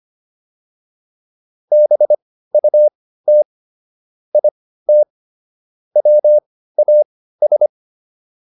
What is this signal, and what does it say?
Morse code, heard as a keyed tone, about 25 wpm, spaced as for 15 wpm, 600 Hz. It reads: BUT IT WAS
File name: WD205 15